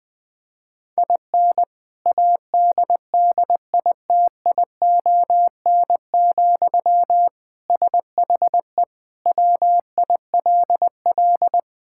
Morse code, heard as a keyed tone, about 19 wpm, 700 Hz. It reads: IN ADDITION, SHE WILL
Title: Sentences from Top 1000 Words - Review Part 6 - 20wpm